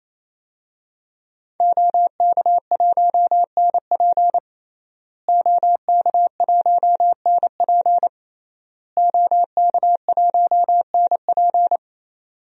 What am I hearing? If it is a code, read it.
Morse code, heard as a keyed tone, about 28 wpm, 700 Hz. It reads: OK1NP OK1NP OK1NP
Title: DXCC-100 Call Signs - 5 Characters - Repeated 3x - 28wpm